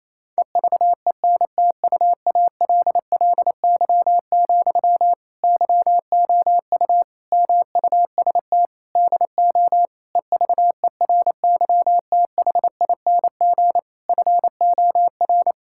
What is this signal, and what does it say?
Morse code, heard as a keyed tone, about 28 wpm, 700 Hz. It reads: EVENTUALLY, YOU MUST DO EVERYTHING FOR
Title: Sentences from Top 1000 Words - Review Part 6 - 28wpm